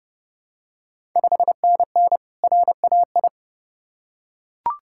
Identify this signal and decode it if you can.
Morse code, heard as a keyed tone, about 30 wpm, 700 Hz. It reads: 5NN RAI
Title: Ontario QSO Party Exchange 30wpm